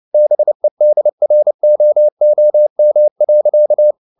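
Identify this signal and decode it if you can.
Morse code, heard as a keyed tone, about 29 wpm, 600 Hz. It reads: BEDROOM.
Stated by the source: Morse code at 29 wpm, sending Ken Follett's novel Never in English, from CW med Gnister.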